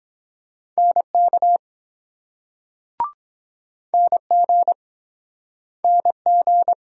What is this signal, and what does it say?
Morse code, heard as a keyed tone, about 23 wpm, 700 Hz. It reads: NK NG NG